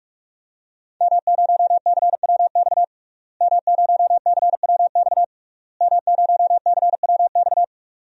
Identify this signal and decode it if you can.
Morse code, heard as a keyed tone, about 43 wpm, 700 Hz. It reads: M0CWX M0CWX M0CWX
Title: DXCC-100 Call Signs - 5 Characters - Repeated 3x - 45wpm